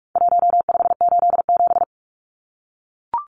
Morse code, heard as a keyed tone, about 44 wpm, 700 Hz. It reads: 1587
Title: Numbers - 4-Digit 45wpm